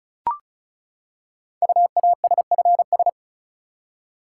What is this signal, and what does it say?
Morse code, heard as a keyed tone, about 36 wpm, 700 Hz. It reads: UASFS
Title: Code Groups - 5 Letters - 35wpm